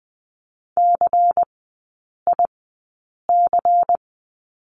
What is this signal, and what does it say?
Morse code, heard as a keyed tone, about 20 wpm, spaced as for 8 wpm, 700 Hz. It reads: CIC
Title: 152 Lesson 25e - Letter p - Three Characters 20wpm